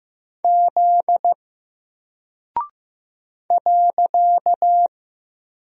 Morse code, heard as a keyed tone, about 15 wpm, 700 Hz. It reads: Z .
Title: Single Character (Letter:Number:Punctuation:Prosign) 15wpm